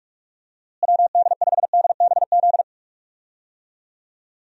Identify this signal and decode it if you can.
Morse code, heard as a keyed tone, about 43 wpm, 700 Hz. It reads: WD5DBZ
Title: USA Call Signs - 6 Characters - Repeated 2x - 45wpm